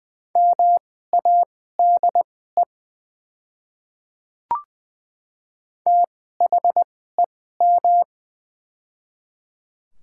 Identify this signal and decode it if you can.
Morse code, heard as a keyed tone, about 21 wpm, spaced as for 15 wpm, 700 Hz. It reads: MADE THEM
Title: 2x Character Spacing - Top 200 Words - 20wpm